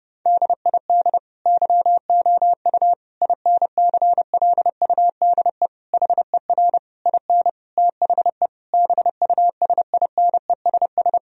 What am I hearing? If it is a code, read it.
Morse code, heard as a keyed tone, about 32 wpm, 700 Hz. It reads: DID YOU INCLUDE HER IN THE BUSINESS
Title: Sentences from Top 1000 Words - Review Part 3 - 30wpm